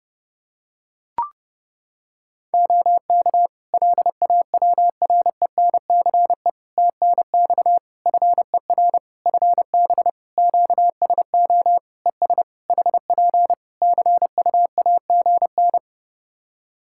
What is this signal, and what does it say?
Morse code, heard as a keyed tone, about 30 wpm, 700 Hz. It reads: OK LAWRENCE TNX FER FB QSO ES HP CUAGN
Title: Practice QSO - Exchange 4 - Ending - 30wpm